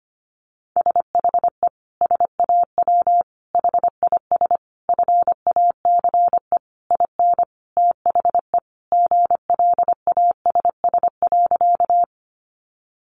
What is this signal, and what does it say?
Morse code, heard as a keyed tone, about 25 wpm, 700 Hz. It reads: SHE SAW HIS FACE IN THE GLASS.
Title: Sentences from Top 1300 Words 25wpm